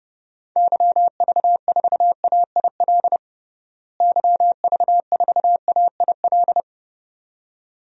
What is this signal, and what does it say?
Morse code, heard as a keyed tone, about 30 wpm, 700 Hz. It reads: YV4AIL YV4AIL
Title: DXCC-100 Call Signs - 6 Characters - Repeated 2x - 30wpm